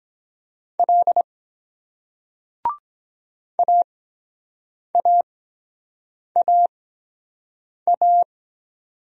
Morse code, heard as a keyed tone, about 23 wpm, spaced as for 7 wpm, 700 Hz. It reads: L AAAA